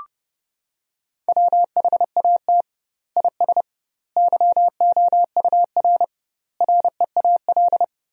Morse code, heard as a keyed tone, about 30 wpm, 700 Hz. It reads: WHAT IS YOUR REAL
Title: 2x Word-Spacing - Sentences from Top 300 Words - 30wpm